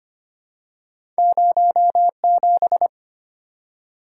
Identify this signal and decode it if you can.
Morse code, heard as a keyed tone, about 25 wpm, 700 Hz. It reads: 07
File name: Two Letter-Number Combinations - 25wpm